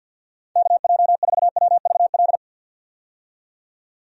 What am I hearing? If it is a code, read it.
Morse code, heard as a keyed tone, about 49 wpm, 700 Hz. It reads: KJ4WVL
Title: USA Call Signs - 6 Characters - 50wpm